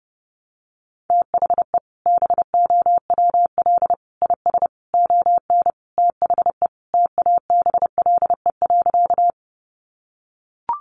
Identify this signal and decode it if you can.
Morse code, heard as a keyed tone, about 30 wpm, 700 Hz. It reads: THE BOWL IS ON THE TABLE.
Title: Sentences from Top 1200 Words 30wpm